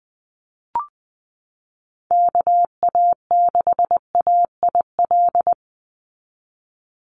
Morse code, heard as a keyed tone, about 19 wpm, 700 Hz. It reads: KA6AIL
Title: Call Signs - 20wpm